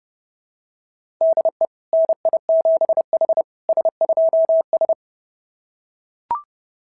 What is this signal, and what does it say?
Morse code, heard as a keyed tone, about 26 wpm, 650 Hz. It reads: DE NI7H S2S
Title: SOTA QSO Practice 30wpm